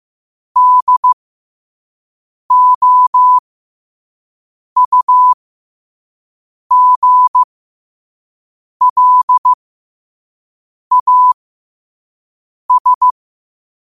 Morse code, heard as a keyed tone, about 15 wpm, spaced as for 5 wpm, 1000 Hz. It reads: DOUGLAS
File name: qotdfiles / 5wpm